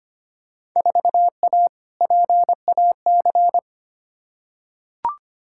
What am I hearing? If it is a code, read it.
Morse code, heard as a keyed tone, about 25 wpm, 700 Hz. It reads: 4A PAC
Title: Field Day Exchanges Encore - 25wpm